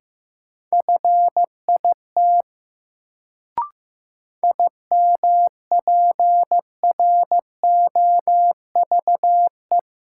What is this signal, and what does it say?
Morse code, heard as a keyed tone, about 16 wpm, 700 Hz. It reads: FIT IMPROVE
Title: Sentences from Top 800 Words - New Word Drill - 15wpm